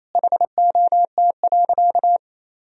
Morse code, heard as a keyed tone, about 28 wpm, 700 Hz. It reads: HOT.
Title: Sentences from Top 500 Words and Abbreviations 28wpm